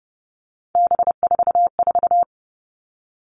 Morse code, heard as a keyed tone, about 30 wpm, 700 Hz. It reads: B44
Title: All Cut Numbers - 3-Digit 30wpm